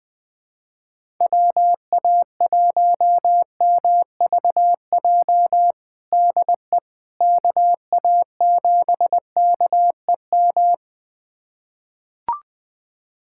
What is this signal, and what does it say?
Morse code, heard as a keyed tone, about 21 wpm, 700 Hz. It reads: WA1MVJ DE KA7KEM
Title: USA Callsign DE Callsign - 6 Characters - 20wpm